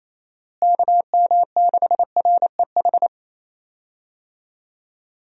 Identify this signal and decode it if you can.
Morse code, heard as a keyed tone, about 28 wpm, 700 Hz. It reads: KM6REH
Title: USA Call Signs - 6 Characters - Repeated 2x - 28wpm